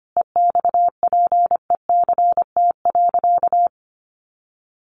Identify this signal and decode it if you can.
Morse code, heard as a keyed tone, about 25 wpm, 700 Hz. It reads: EXPECT.
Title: Sentences from Top 1100 Words 25wpm